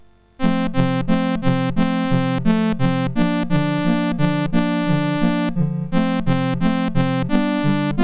There is no code.